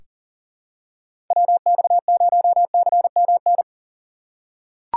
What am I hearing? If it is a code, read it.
Morse code, heard as a keyed tone, about 39 wpm, 700 Hz. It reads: WX0CMN
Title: USA Call Signs - 6 Characters - 6x Character Spacing - 40wpm